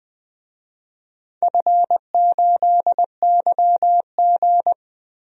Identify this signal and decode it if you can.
Morse code, heard as a keyed tone, about 20 wpm, 700 Hz. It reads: F8YG
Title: DXCC-100 Call Signs - 4 Characters - 20wpm